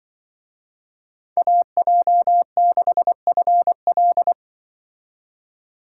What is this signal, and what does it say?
Morse code, heard as a keyed tone, about 25 wpm, 700 Hz. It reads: AJ6FL
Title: Speed Racing - USA 2x2 Call Signs 20wpm